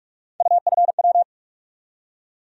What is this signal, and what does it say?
Morse code, heard as a keyed tone, about 47 wpm, 700 Hz. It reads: UFW
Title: USA Call Sign Postfixes - 3 Letters 45wpm